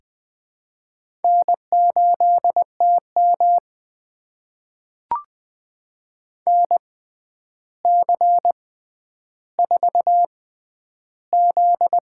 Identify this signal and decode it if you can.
Morse code, heard as a keyed tone, about 20 wpm, spaced as for 10 wpm, 700 Hz. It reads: N8TM NC4Z
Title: USA Call Signs - 4 Characters - 6x Character Spacing - 20wpm